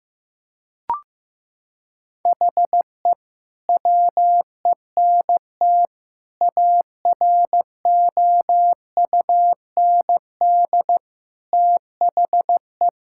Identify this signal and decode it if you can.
Morse code, heard as a keyed tone, about 16 wpm, 700 Hz. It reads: HE WENT AROUND THE
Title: Sentences from Top 1000 Words - Review Part 5 - 15wpm